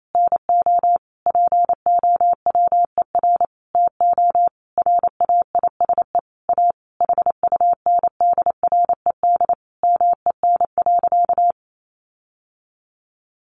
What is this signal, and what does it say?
Morse code, heard as a keyed tone, about 28 wpm, 700 Hz. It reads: NO POWER TO RAISE A HUNDRED MEN.
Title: Sentences from Top 1400 Words 28wpm